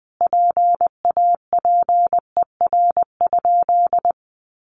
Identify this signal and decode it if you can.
Morse code, heard as a keyed tone, about 20 wpm, 700 Hz. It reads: PAPER?
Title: Sentences from Top 1300 Words 20wpm